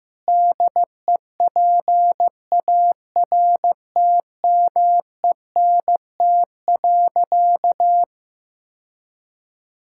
Morse code, heard as a keyed tone, about 15 wpm, 700 Hz. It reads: DEPARTMENT.
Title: Sentences from Top 1000 Words - Review Part 5 - 15wpm